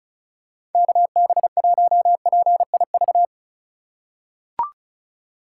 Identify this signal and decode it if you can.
Morse code, heard as a keyed tone, about 36 wpm, 700 Hz. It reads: KB1PIV
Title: USA Call Signs - 6 Characters - 4x Character Spacing - 35wpm